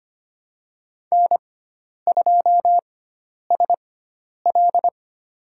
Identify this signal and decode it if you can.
Morse code, heard as a keyed tone, about 25 wpm, spaced as for 10 wpm, 700 Hz. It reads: N2SL
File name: Lesson 16g - Letter c - Call Signs 25wpm